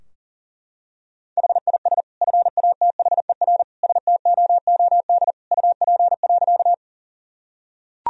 Morse code, heard as a keyed tone, about 40 wpm, 700 Hz. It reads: HIS FATHER STOOD UP.
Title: Sentences from Top 1000 Words - Review Part 2 - 40wpm